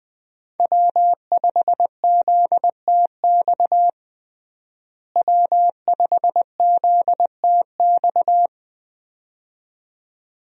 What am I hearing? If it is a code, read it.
Morse code, heard as a keyed tone, about 20 wpm, 700 Hz. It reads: W5ZTX W5ZTX
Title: USA Call Signs - 5 Characters - Repeated 2x - 20wpm